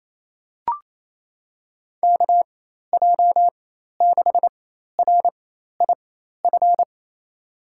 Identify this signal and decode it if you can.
Morse code, heard as a keyed tone, about 27 wpm, spaced as for 13 wpm, 700 Hz. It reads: KJ6RIF
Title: USA Call Signs - 6 Characters - 4x Character Spacing - 28wpm